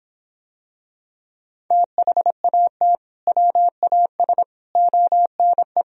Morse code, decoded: THAT WAS ONE

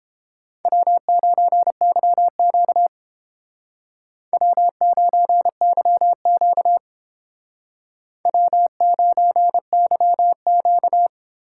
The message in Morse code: W9YQ W9YQ W9YQ

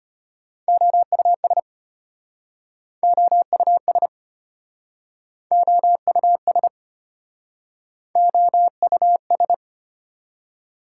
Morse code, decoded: OUS OUS OUS OUS

